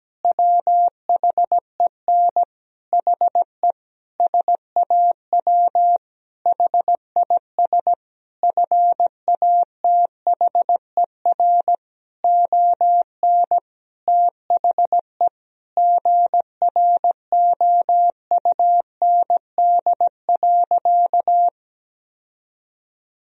Morse code, heard as a keyed tone, about 17 wpm, 700 Hz. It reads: WHEN HE SAW HIS FATHER ON THE GROUND.
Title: Sentences from Top 1000 Words - Review Part 3 - 17wpm